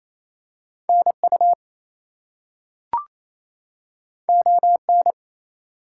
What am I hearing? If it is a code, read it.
Morse code, heard as a keyed tone, about 28 wpm, 700 Hz. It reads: NU ON
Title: US States and Canadian Provinces 28wpm